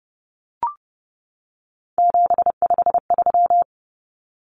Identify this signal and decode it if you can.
Morse code, heard as a keyed tone, about 30 wpm, 700 Hz. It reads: 753
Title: Common Cut Numbers - 3-Digit 30wpm